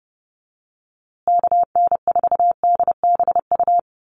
Morse code, heard as a keyed tone, about 30 wpm, 700 Hz. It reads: KN4DBU
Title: USA Call Signs Repeated 3x 30wpm